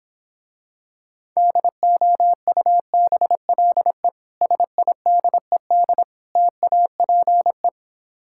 Morse code, heard as a keyed tone, about 26 wpm, 700 Hz. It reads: DOUBLE SIDED TAPE